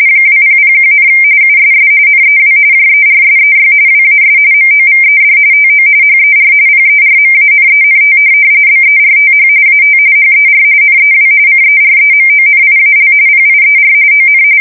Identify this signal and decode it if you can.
RTTY, about 45.45 baud. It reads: 2001, USS NEW JERSEY WAS MOVED TO HER PERMANENT BERTH AT HER NEW CEREMONIAL PIER ON
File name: fsk170 / USS-New-Jersey